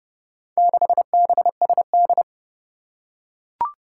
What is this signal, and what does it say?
Morse code, heard as a keyed tone, about 30 wpm, 700 Hz. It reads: 6BSD
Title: Four Challenging Letter-Number Combinations 30wpm